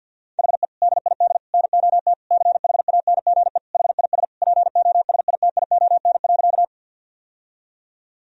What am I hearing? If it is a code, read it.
Morse code, heard as a keyed tone, about 52 wpm, 700 Hz. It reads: HE DID NOT CHANGE HIS POSITION.
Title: Sentences from Top 1000 Words - Review Part 1 - 50wpm